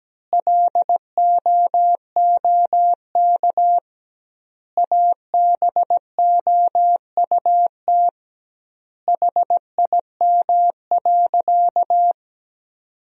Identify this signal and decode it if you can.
Morse code, heard as a keyed tone, about 17 wpm, 700 Hz. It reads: LOOK ABOUT HIM.
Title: 2x Word-Spacing - Sentences from Top 200 Words - 17wpm